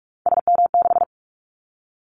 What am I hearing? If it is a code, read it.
Morse code, heard as a keyed tone, about 44 wpm, 700 Hz. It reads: SM6
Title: Callsign Prefixes for Top 25 DXCC - Repeated 3x - 45wpm